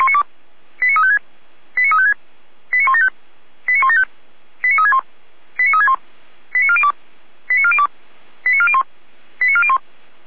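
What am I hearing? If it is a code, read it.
Five-tone selective call (ZVEI), heard as quick runs of five tones, tones of about 80 ms. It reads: 89356 89356 89175 89175 89361 89361 89402 89402 89401 89401
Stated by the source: ZVEI five-tone call received with an SDR.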